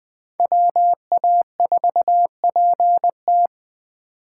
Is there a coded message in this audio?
Yes: Morse code from 0.4 to 3.5 s, heard as a keyed tone, about 20 wpm, 700 Hz. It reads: WA4PT